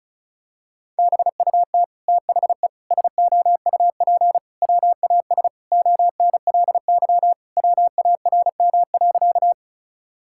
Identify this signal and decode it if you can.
Morse code, heard as a keyed tone, about 34 wpm, 700 Hz. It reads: BUT THE SOUP WAS ONLY WARM.
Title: Sentences from Top 1000 Words - Review Part 6 - 35wpm